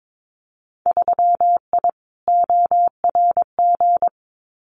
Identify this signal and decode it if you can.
Morse code, heard as a keyed tone, about 22 wpm, 700 Hz. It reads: 3I ORG
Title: Winter Field Day Exchanges Encore 22wpm